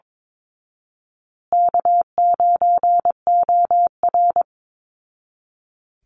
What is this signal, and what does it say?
Morse code, heard as a keyed tone, about 22 wpm, 700 Hz. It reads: K9OR
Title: Call Signs Repeated 2x - 22wpm